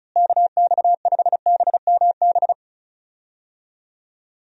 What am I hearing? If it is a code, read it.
Morse code, heard as a keyed tone, about 34 wpm, 700 Hz. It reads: KX5BMB
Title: USA Call Signs - 6 Characters - Repeated 2x - 35wpm